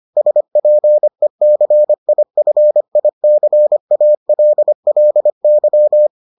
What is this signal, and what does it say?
Morse code, heard as a keyed tone, about 25 wpm, 600 Hz. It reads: SPECIFICALLY